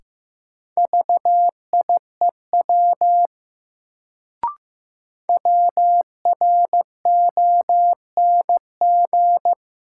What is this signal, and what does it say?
Morse code, heard as a keyed tone, about 15 wpm, 700 Hz. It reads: VIEW WRONG
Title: Sentences from Top 900 Words - New Word Drill - 15wpm